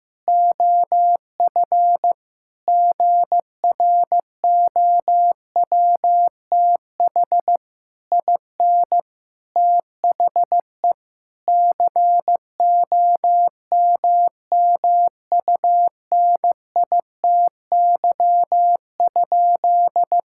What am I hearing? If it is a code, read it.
Morse code, heard as a keyed tone, about 15 wpm, 700 Hz. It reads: OF GROWTH IN THE COMMUNITY?
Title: Sentences from Top 1000 Words - Review Part 5 - 15wpm